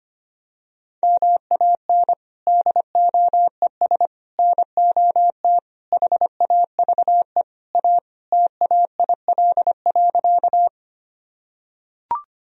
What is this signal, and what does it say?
Morse code, heard as a keyed tone, about 25 wpm, 700 Hz. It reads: MAN DOES NOT HAVE A TAIL.
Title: Sentences from Top 1000 Words - Review Part 3 - 25wpm